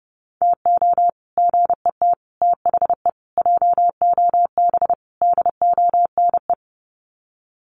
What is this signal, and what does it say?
Morse code, heard as a keyed tone, about 28 wpm, 700 Hz. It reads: TO GET THE JOB DONE
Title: Sets of 5 Words Encore2 30wpm